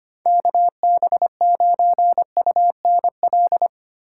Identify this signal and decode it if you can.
Morse code, heard as a keyed tone, about 25 wpm, 700 Hz. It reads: KB9UNL
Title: USA Call Signs - 6 Characters - 6x Character Spacing - 25wpm